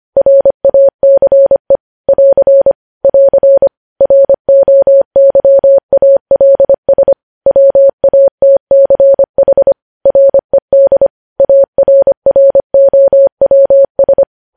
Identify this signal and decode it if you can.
Morse code, heard as a keyed tone, about 25 wpm, 550 Hz. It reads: RACE <AR> <AR> ROYALS WATCH RED ARROWS
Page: News Headlines in Morse Code at 25 WPM